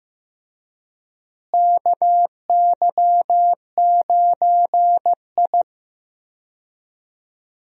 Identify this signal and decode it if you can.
Morse code, heard as a keyed tone, about 15 wpm, 700 Hz. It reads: KY9I
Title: USA Call Signs - 4 Characters - Repeated 3x - 15wpm